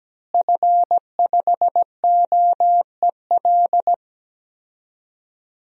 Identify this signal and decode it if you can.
Morse code, heard as a keyed tone, about 17 wpm, 700 Hz. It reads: F5OEL
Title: DXCC-100 Call Signs - 5 Characters - Repeated 2x - 17wpm